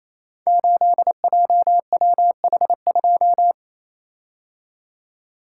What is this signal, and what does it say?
Morse code, heard as a keyed tone, about 27 wpm, 700 Hz. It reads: 8JWH2
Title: Code Groups - 5 Letter:Numbers - 28wpm